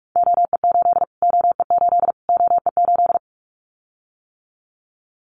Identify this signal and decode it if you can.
Morse code, heard as a keyed tone, about 44 wpm, 700 Hz. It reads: OE8 OE8 OE8
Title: Callsign Prefixes for Top 25 DXCC - Repeated 3x - 45wpm